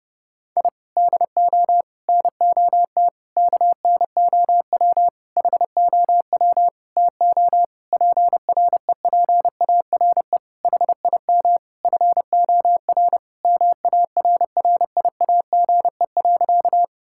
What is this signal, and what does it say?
Morse code, heard as a keyed tone, about 30 wpm, 700 Hz. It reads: I DO NOT KNOW HOW TO PREPARE HIM FOR MARRIAGE.
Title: Sentences from Top 1000 Words - Review Part 6 - 30wpm